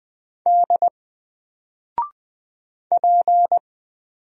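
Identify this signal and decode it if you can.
Morse code, heard as a keyed tone, about 20 wpm, 700 Hz. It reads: D P
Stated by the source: Single letter and number Morse drill at 20 wpm, from N4LCD.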